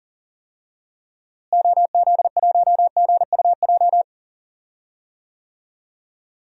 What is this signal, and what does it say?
Morse code, heard as a keyed tone, about 40 wpm, 700 Hz. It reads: OZ1GUJ
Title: DXCC-100 Call Signs - 6 Characters - Repeated 3x - 40wpm